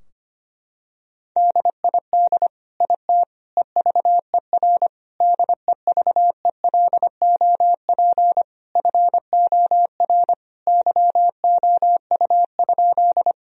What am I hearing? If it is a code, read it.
Morse code, heard as a keyed tone, about 25 wpm, 700 Hz. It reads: DID IT EVER DEVELOP FOR YOU?
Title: Sentences from Top 1000 Words - Review Part 2 - 25wpm